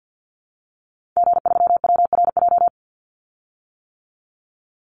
Morse code, heard as a keyed tone, about 50 wpm, 700 Hz. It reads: G3WRJ
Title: Call Signs - 50wpm